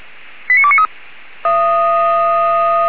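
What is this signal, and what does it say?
Five-tone selective call (ZVEI), heard as quick runs of five tones, tones of about 70 ms. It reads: 89293
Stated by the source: ZVEI five-tone call received on SDR.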